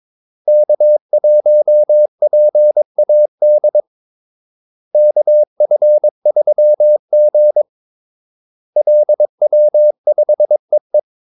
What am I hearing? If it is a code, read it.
Morse code, heard as a keyed tone, about 22 wpm, 600 Hz. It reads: K1PAD KF3G LW5EE
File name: Callsigns-22wpm